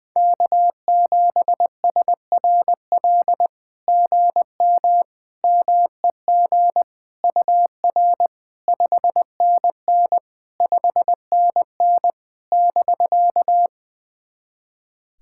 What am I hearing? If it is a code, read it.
Morse code, heard as a keyed tone, about 20 wpm, 700 Hz. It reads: K7SRL GM MEG UR 5NN 5NN <BK>
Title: SOTA QSO Practice 20wpm